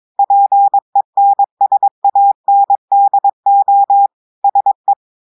Morse code, heard as a keyed tone, about 22 wpm, 800 Hz. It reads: PENSANDO SE
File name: audios / portugais / portugais texte en clair 2 vitesse 22 mots minutes